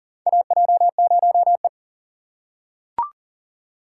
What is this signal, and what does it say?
Morse code, heard as a keyed tone, about 40 wpm, 700 Hz. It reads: AJ0E
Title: USA Call Signs - 4 Characters - 3x Character Spacing - 40wpm